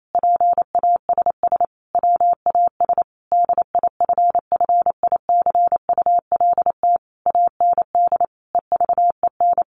Morse code, heard as a keyed tone, about 28 wpm, 700 Hz. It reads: PASS WAS DIFFICULT AND EVEN
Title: Sentences from Top 1200 Words 28wpm